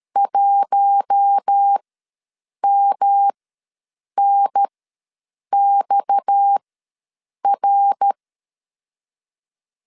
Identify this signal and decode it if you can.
Morse code, heard as a keyed tone, about 13 wpm, spaced as for 7 wpm, 800 Hz. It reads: 1MNXR
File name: LEZIONI / 7 GF9Z / 7 G ws 12wpm 3140